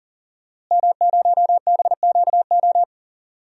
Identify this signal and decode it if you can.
Morse code, heard as a keyed tone, about 39 wpm, 700 Hz. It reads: M0BQO
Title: DXCC-100 Call Signs - 5 Characters - Repeated 2x - 40wpm